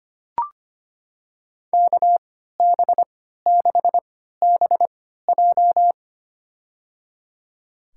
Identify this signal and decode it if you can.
Morse code, heard as a keyed tone, about 25 wpm, spaced as for 14 wpm, 700 Hz. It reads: KB6BJ
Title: USA Call Signs - 5 Characters - 3x Character Spacing - 25wpm